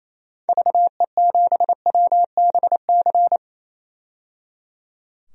Morse code, heard as a keyed tone, about 28 wpm, 700 Hz. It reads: VE7WBC
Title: DXCC-100 Call Signs - 6 Characters - 28wpm